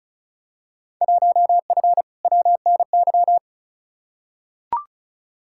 Morse code, heard as a keyed tone, about 36 wpm, 700 Hz. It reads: 1F WNY
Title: Field Day Exchanges - 35wpm